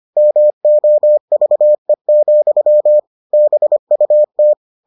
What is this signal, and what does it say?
Morse code, heard as a keyed tone, about 25 wpm, 600 Hz. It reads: MOVE, BUT